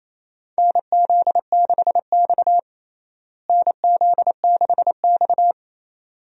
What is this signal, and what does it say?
Morse code, heard as a keyed tone, about 28 wpm, 700 Hz. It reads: NZ6X NZ6X
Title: USA Call Signs - 4 Characters - Repeated 2x - 28wpm